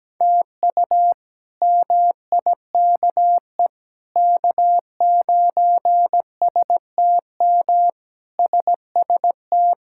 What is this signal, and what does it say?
Morse code, heard as a keyed tone, about 17 wpm, 700 Hz. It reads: TU MIKE K9STM SST
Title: K1USN SST QSO Practice - 17wpm